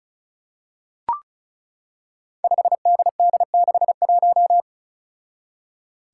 Morse code, heard as a keyed tone, about 34 wpm, 700 Hz. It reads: 5DD61
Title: Five Challenging Letter-Number Combinations 35wpm